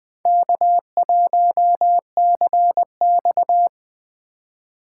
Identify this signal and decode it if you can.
Morse code, heard as a keyed tone, about 20 wpm, 700 Hz. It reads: K1CX
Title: K1USN SST QSO Practice - 20wpm